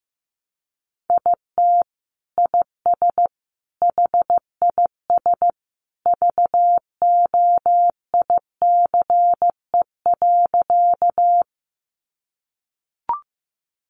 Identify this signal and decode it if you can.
Morse code, heard as a keyed tone, about 15 wpm, 700 Hz. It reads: IT IS HIS VOICE.
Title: Sentences from Top 500 Words 15wpm